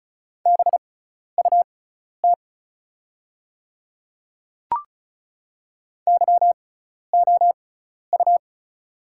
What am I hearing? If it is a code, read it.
Morse code, heard as a keyed tone, about 36 wpm, spaced as for 12 wpm, 700 Hz. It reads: BUT YOU
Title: 6x Character Spacing - Top 100 Words - 35wpm